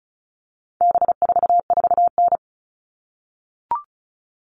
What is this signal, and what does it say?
Morse code, heard as a keyed tone, about 34 wpm, 700 Hz. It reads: B44N
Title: All Cut Numbers - 4-Digit 35wpm